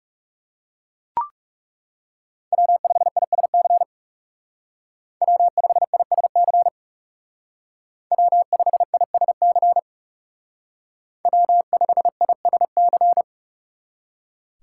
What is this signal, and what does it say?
Morse code, heard as a keyed tone, about 36 wpm, 700 Hz. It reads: W5ISC W5ISC W5ISC W5ISC
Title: Speed Racing - USA 1x3 Call Signs 30wpm